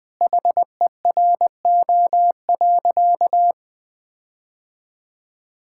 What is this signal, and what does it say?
Morse code, heard as a keyed tone, about 20 wpm, 700 Hz. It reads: HERO.
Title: Sentences with 4-Word Phrases - 20wpm